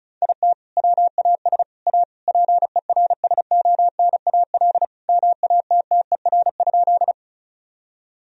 Morse code, heard as a keyed tone, about 36 wpm, 700 Hz. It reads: IT WAS A PERSONAL MATTER?
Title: Sentences from Top 1000 Words - Review Part 4 - 35wpm